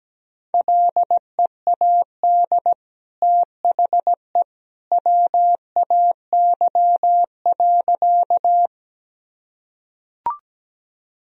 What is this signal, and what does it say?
Morse code, heard as a keyed tone, about 17 wpm, 700 Hz. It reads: LEAD THE WAY.
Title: Sentences from Top 1000 Words - Review Part 2 - 17wpm